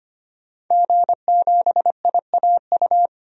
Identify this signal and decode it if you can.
Morse code, heard as a keyed tone, about 25 wpm, 700 Hz. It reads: G7IAU
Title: DXCC-100 Call Signs - 5 Characters - 25wpm